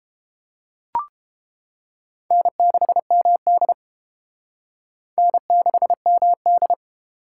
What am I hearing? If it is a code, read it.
Morse code, heard as a keyed tone, about 30 wpm, 700 Hz. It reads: N6MD N6MD